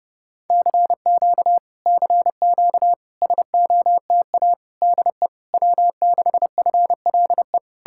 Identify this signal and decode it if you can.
Morse code, heard as a keyed tone, about 33 wpm, 700 Hz. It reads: CQ CQ SOTA DE W6FLE